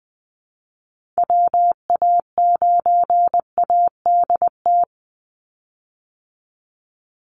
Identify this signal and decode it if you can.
Morse code, heard as a keyed tone, about 20 wpm, 700 Hz. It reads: WA9ADT
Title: Call Signs Repeated 3x - 20wpm